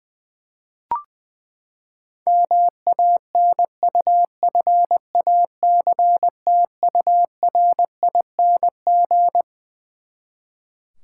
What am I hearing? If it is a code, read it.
Morse code, heard as a keyed tone, about 21 wpm, 700 Hz. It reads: MANUFACTURING
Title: Practice QSO - Exchange 3 - More Info - 20wpm